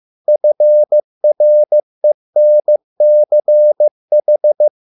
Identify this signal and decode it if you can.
Morse code, heard as a keyed tone, about 15 wpm, 600 Hz. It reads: FRENCH